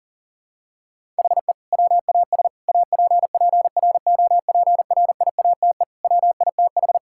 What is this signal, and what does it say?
Morse code, heard as a keyed tone, about 40 wpm, 700 Hz. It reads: HE WAS APPROPRIATE WITH
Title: Sentences from Top 1000 Words - Review Part 5 - 40wpm